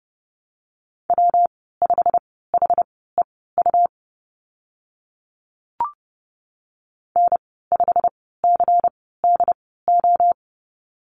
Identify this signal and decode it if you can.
Morse code, heard as a keyed tone, about 30 wpm, spaced as for 17 wpm, 700 Hz. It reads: W5HEU N5CDO
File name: Lesson 19g - Letter w - Call Signs 30wpm